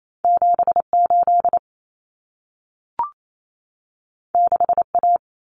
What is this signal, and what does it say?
Morse code, heard as a keyed tone, about 28 wpm, 700 Hz. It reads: 78 6A
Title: Common Cut Numbers - 2-Digit 28wpm